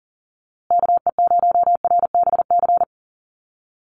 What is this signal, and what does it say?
Morse code, heard as a keyed tone, about 40 wpm, 700 Hz. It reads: KE0RBC
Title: USA Call Signs Repeated 2x 40wpm